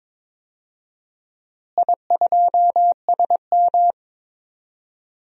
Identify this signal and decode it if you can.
Morse code, heard as a keyed tone, about 22 wpm, 700 Hz. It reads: I2SM
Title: DXCC-100 Call Signs - 4 Characters - Repeated 3x - 22wpm